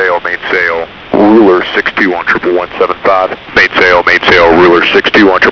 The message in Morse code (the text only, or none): none